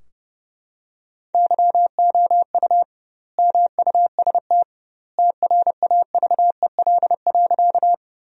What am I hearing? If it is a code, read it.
Morse code, heard as a keyed tone, about 30 wpm, 700 Hz. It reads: YOU MUST TRAVEL.
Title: 2x Word-Spacing - Sentences from Top 400 Words - 30wpm